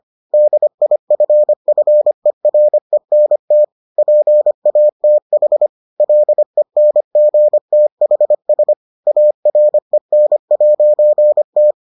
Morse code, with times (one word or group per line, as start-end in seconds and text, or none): 0.3-3.6 DIFFERENT
4.0-5.7 PATH
6.0-8.7 LENGTHS
9.1-11.7 AREN'T